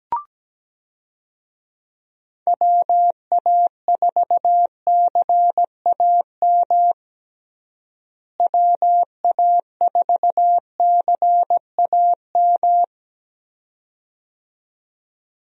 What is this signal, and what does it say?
Morse code, heard as a keyed tone, about 17 wpm, 700 Hz. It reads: WA4CAM WA4CAM
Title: USA Call Signs - 6 Characters - Repeated 2x - 17wpm